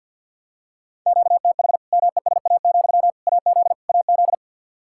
Morse code, heard as a keyed tone, about 50 wpm, 700 Hz. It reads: QTH MESA, AZ AZ
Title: Practice QSO - Exchange 1 - Beginning - 50wpm